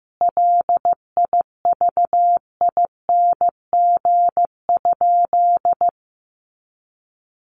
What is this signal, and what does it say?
Morse code, heard as a keyed tone, about 15 wpm, 700 Hz. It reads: LIVING?
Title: Sentences from Top 1100 Words 15wpm